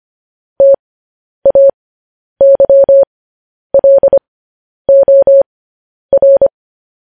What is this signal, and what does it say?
Morse code, heard as a keyed tone, about 25 wpm, spaced as for 10 wpm, 550 Hz. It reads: TAYLOR